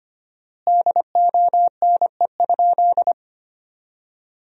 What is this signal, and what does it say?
Morse code, heard as a keyed tone, about 25 wpm, 700 Hz. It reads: DONE?
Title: 4x Word-Spacing - Sentences from Top 500 Words - 25wpm